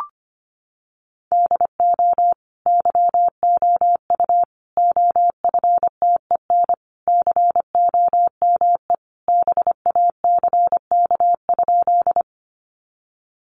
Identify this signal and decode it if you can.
Morse code, heard as a keyed tone, about 25 wpm, 700 Hz. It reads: DO YOU OFTEN COME BACK?
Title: Sentences from Top 300 Words 25wpm